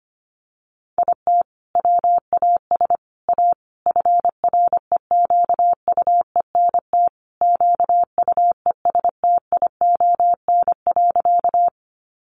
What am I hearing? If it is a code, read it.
Morse code, heard as a keyed tone, about 25 wpm, 700 Hz. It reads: IT WAS A FREQUENT QUESTION.
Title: Sentences from Top 1400 Words 25wpm